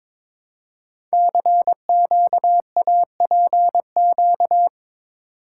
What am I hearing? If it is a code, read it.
Morse code, heard as a keyed tone, about 23 wpm, 700 Hz. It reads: CQAPQ
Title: Code Groups - 5 Letters - 22wpm